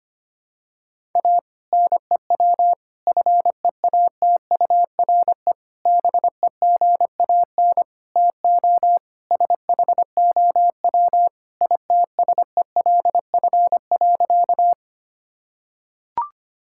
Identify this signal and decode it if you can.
Morse code, heard as a keyed tone, about 25 wpm, 700 Hz. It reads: A NEW FEATURE BEGAN TO SHOW ITSELF.
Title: Sentences from Top 1000 Words - Review Part 5 - 25wpm